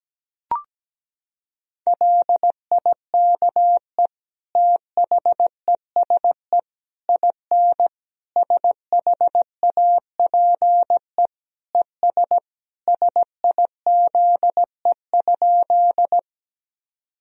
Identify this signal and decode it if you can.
Morse code, heard as a keyed tone, about 17 wpm, 700 Hz. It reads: LIKE THESE IN SHAPE ES SIZE?
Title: Sentences from Top 500 Words and Abbreviations 17wpm